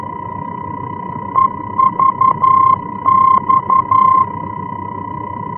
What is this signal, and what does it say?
Morse code, heard as a keyed tone, about 11 wpm, 1050 Hz. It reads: EVX
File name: france / mx / EVX-F-VOR